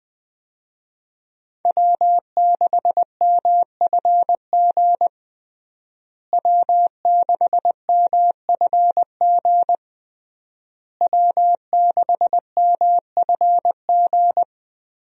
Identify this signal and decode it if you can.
Morse code, heard as a keyed tone, about 21 wpm, 700 Hz. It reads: W6MFG W6MFG W6MFG
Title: USA Call Signs - 5 Characters - Repeated 3x - 20wpm